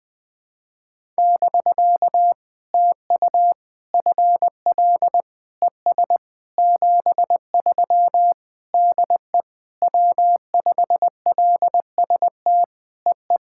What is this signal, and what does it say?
Morse code, heard as a keyed tone, about 20 wpm, 700 Hz. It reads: <BK> TU FL ES 73 DE W5LST EE
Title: POTA QSO Practice - 20wpm